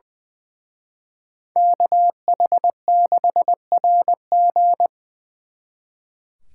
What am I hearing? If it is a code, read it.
Morse code, heard as a keyed tone, about 24 wpm, 700 Hz. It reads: KH6RG
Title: Speed Racing - USA Call Signs 20wpm